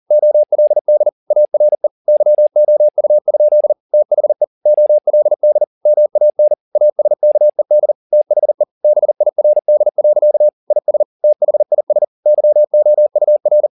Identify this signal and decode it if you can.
Morse code, heard as a keyed tone, about 40 wpm, 600 Hz. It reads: OLD ARE YOU? THE OLD MAN ASKED THE BIRD. IS THIS YOUR